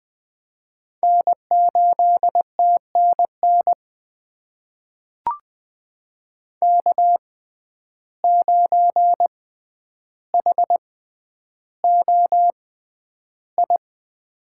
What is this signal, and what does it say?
Morse code, heard as a keyed tone, about 20 wpm, spaced as for 10 wpm, 700 Hz. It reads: N8TNN K9HOI
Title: USA Call Signs - 5 Characters - 6x Character Spacing - 20wpm